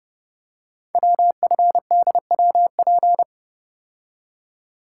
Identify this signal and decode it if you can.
Morse code, heard as a keyed tone, about 30 wpm, 700 Hz. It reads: WFDWP
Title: Code Groups - 5 Letters - 30wpm